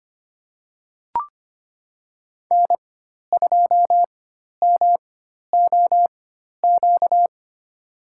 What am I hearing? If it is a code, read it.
Morse code, heard as a keyed tone, about 25 wpm, spaced as for 12 wpm, 700 Hz. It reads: N2MOQ
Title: USA Call Signs - 4x Character Spacing - 25wpm